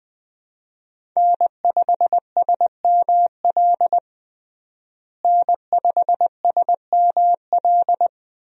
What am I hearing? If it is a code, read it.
Morse code, heard as a keyed tone, about 20 wpm, 700 Hz. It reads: N5SML N5SML
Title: USA Call Signs - 5 Characters - Repeated 2x - 20wpm